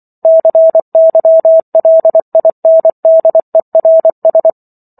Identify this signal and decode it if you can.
Morse code, heard as a keyed tone, about 24 wpm, 650 Hz. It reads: CYLINDERS